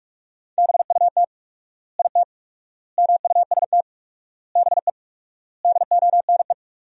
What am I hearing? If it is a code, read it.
Morse code, heard as a keyed tone, about 47 wpm, 700 Hz. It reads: BUT IT MUST BE DONE
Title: 4x Word-Spacing - Sentences from Top 500 Words - 45wpm